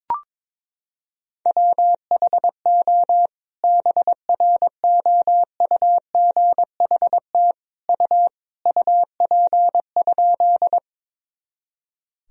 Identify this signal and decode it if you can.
Morse code, heard as a keyed tone, about 22 wpm, 700 Hz. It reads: WHO BROUGHT U UP?
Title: Sentences from Top 500 Words and Abbreviations 22wpm